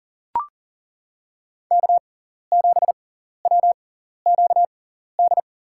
Morse code, heard as a keyed tone, about 40 wpm, spaced as for 14 wpm, 700 Hz. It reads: K7WQD
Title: USA Call Signs - 6x Character Spacing - 40wpm